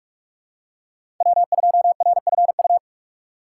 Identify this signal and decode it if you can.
Morse code, heard as a keyed tone, about 45 wpm, 700 Hz. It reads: W2RFU
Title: USA Call Signs - 5 Characters - 3x Character Spacing - 45wpm